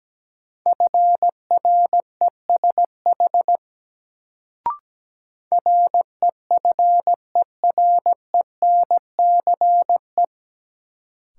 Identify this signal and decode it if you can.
Morse code, heard as a keyed tone, about 17 wpm, 700 Hz. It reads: FRESH REFERENCE
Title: Sentences from Top 1200 Words - New Word Drill - 17wpm